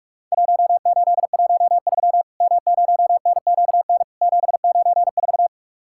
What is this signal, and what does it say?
Morse code, heard as a keyed tone, about 44 wpm, 700 Hz. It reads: 1813 M0NQN 794
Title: Worked All Europe DX Contest - QTC Exchange - 45wpm